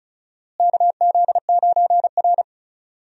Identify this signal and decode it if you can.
Morse code, heard as a keyed tone, about 36 wpm, 700 Hz. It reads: KZ9R
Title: USA Call Signs - 4 Characters - Repeated 3x - 35wpm